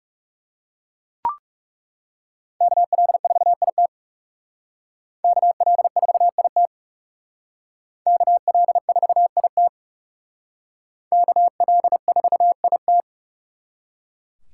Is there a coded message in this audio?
Yes: Morse code from 2.6 to 13.0 s, heard as a keyed tone, about 39 wpm, 700 Hz. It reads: KL4IT KL4IT KL4IT KL4IT